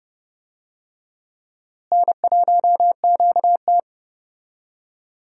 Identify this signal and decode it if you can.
Morse code, heard as a keyed tone, about 29 wpm, 700 Hz. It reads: N1QT
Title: USA Call Signs - 4 Characters - Repeated 2x - 30wpm